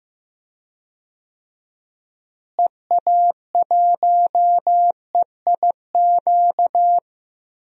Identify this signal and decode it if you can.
Morse code, heard as a keyed tone, about 15 wpm, 700 Hz. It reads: EA1EIQ
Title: DXCC-100 Call Signs - 6 Characters - Repeated 3x - 15wpm